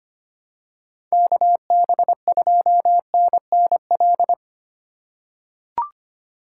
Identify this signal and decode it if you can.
Morse code, heard as a keyed tone, about 25 wpm, 700 Hz. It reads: KB2NNL
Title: USA Call Signs - 2x Character Spacing - 25wpm